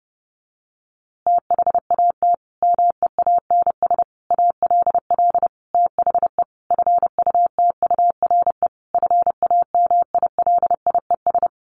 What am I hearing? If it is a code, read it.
Morse code, heard as a keyed tone, about 30 wpm, 700 Hz. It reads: THAT MEANS ALL THE FUTURE FAMILIES
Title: A Sound Of Thunder 30wpm